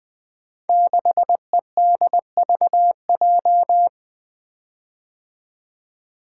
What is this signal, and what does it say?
Morse code, heard as a keyed tone, about 20 wpm, 700 Hz. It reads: 6EDVJ
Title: Code Groups - 5 Letter:Numbers - 20wpm